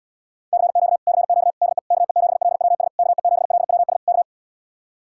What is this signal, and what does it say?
Morse code, heard as a keyed tone, about 77 wpm, 700 Hz. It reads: CQ CQ DE KE7LWS KE7LWS K